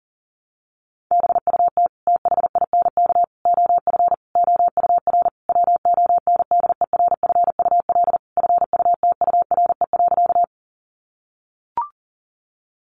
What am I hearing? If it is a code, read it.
Morse code, heard as a keyed tone, about 37 wpm, 700 Hz. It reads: BUT THINK OF OUR WONDERFUL FUTURE.
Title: Sentences from Top 1100 Words 40wpm